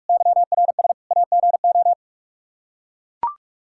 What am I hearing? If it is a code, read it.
Morse code, heard as a keyed tone, about 42 wpm, 700 Hz. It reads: YRS AGO
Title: K1USN SST QSO Practice 45wpm